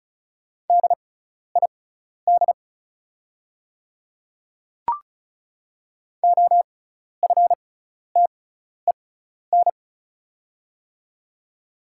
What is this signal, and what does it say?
Morse code, heard as a keyed tone, about 36 wpm, spaced as for 12 wpm, 690 Hz. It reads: DID OFTEN
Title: 6x Character Spacing - Top 300 Words - 35wpm